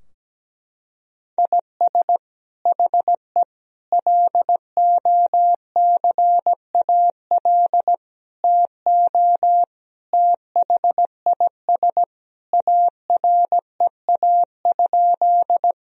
Morse code, heard as a keyed tone, about 17 wpm, 700 Hz. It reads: IS HE LOCAL TO THIS AREA?
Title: Sentences from Top 1000 Words - Review Part 3 - 17wpm